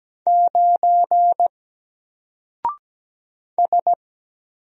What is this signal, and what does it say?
Morse code, heard as a keyed tone, about 17 wpm, 700 Hz. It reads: 9 S
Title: Single Letter-Number Alt 17wpm